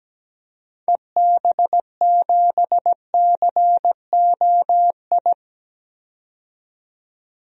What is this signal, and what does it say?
Morse code, heard as a keyed tone, about 17 wpm, 700 Hz. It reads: EB7COI
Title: DXCC-100 Call Signs - 6 Characters - 17wpm